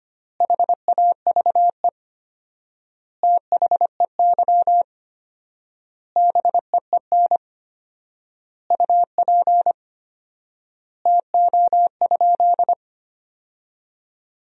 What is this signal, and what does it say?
Morse code, heard as a keyed tone, about 25 wpm, 700 Hz. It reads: HAVE THEY BEEN UP TO?
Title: 4x Word-Spacing - Sentences from Top 100 Words - 25wpm